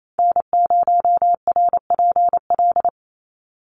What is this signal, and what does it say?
Morse code, heard as a keyed tone, about 28 wpm, 700 Hz. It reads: N0RPL
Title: USA Call Signs 28wpm